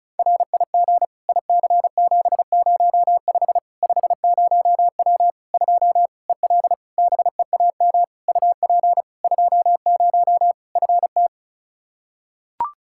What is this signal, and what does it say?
Morse code, heard as a keyed tone, about 34 wpm, 700 Hz. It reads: RIG IC705 50W 2 EL BEAM UP 20 FT
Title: Practice QSO - Exchange 2 - Info - 35wpm